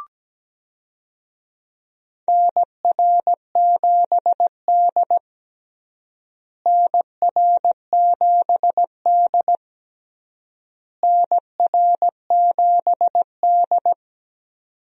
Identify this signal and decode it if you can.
Morse code, heard as a keyed tone, about 17 wpm, 700 Hz. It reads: NR7D NR7D NR7D
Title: USA Call Signs - 4 Characters - Repeated 3x - 17wpm